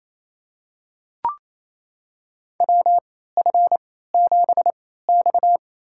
Morse code, heard as a keyed tone, about 28 wpm, spaced as for 16 wpm, 700 Hz. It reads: WF7X
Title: USA Call Signs - 4 Characters - 3x Character Spacing - 28wpm